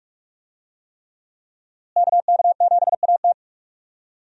Morse code, heard as a keyed tone, about 47 wpm, 700 Hz. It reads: KK7AT
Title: USA Call Signs - 5 Characters - Repeated 3x - 45wpm